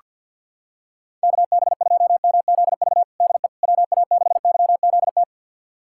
Morse code, heard as a keyed tone, about 47 wpm, 700 Hz. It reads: KB2MZV DE WA6YZT
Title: USA Callsign DE Callsign - 6 Characters - 50wpm